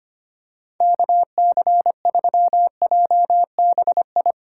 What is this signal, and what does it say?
Morse code, heard as a keyed tone, about 25 wpm, 700 Hz. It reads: KC3JBI
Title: USA Call Signs - 6 Characters - 4x Character Spacing - 25wpm